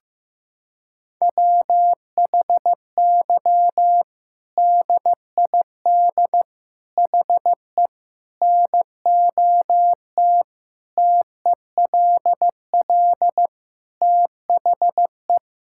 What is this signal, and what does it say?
Morse code, heard as a keyed tone, about 15 wpm, 700 Hz. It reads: WHY DID HE NOT TELL THE
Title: Sentences from Top 1600 Words 15wpm